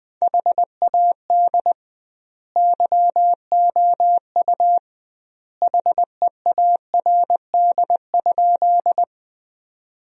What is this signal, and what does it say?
Morse code, heard as a keyed tone, about 20 wpm, 700 Hz. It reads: HAD YOU HEARD?
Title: 2x Word-Spacing - Sentences from Top 400 Words - 20wpm